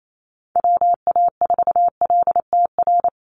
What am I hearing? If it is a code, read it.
Morse code, heard as a keyed tone, about 28 wpm, 700 Hz. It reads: WA4LTR
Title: USA Call Signs 28wpm